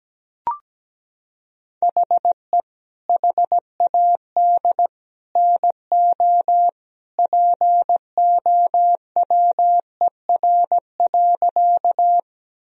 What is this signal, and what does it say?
Morse code, heard as a keyed tone, about 17 wpm, 700 Hz. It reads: HE HAD NO POWER.
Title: Sentences from Top 1000 Words - Review Part 2 - 17wpm